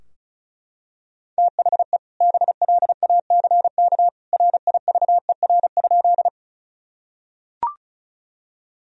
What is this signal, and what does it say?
Morse code, heard as a keyed tone, about 36 wpm, 700 Hz. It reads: THE BLACK RIVER?
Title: Sentences from Top 1000 Words - Review Part 2 - 35wpm